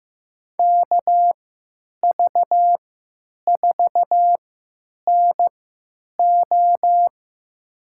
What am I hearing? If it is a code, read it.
Morse code, heard as a keyed tone, about 15 wpm, spaced as for 9 wpm, 700 Hz. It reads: KV4NO
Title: USA Call Signs - 5 Characters - 3x Character Spacing - 15wpm